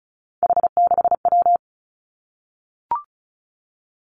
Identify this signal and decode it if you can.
Morse code, heard as a keyed tone, about 36 wpm, 700 Hz. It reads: H6W
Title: Three Challenging Letter-Number Combinations 35wpm